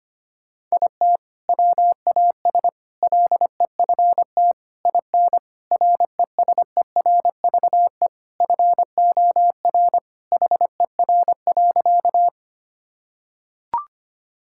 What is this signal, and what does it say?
Morse code, heard as a keyed tone, about 25 wpm, 700 Hz. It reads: IT WAS LEFT IN RESERVE FOR HER.
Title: Sentences from Top 1000 Words - Review Part 6 - 25wpm